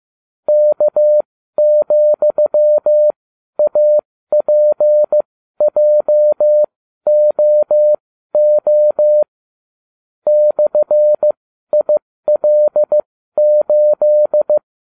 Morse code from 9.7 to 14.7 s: /IL8

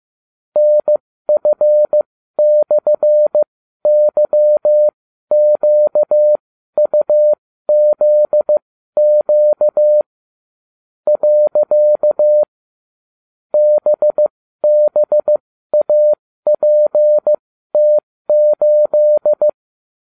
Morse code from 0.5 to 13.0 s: NF/YQUZQ .

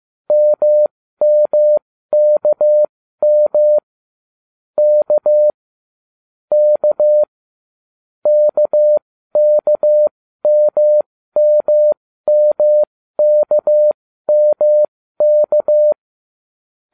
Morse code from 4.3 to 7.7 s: K K

